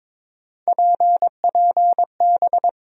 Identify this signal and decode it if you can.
Morse code, heard as a keyed tone, about 22 wpm, 700 Hz. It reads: PPB
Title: USA Call Sign Postfixes - 3 Letters 22wpm